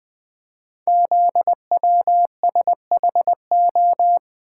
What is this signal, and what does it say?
Morse code, heard as a keyed tone, about 20 wpm, 700 Hz. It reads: ZWSHO
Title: Code Groups - 5 Letters - 20wpm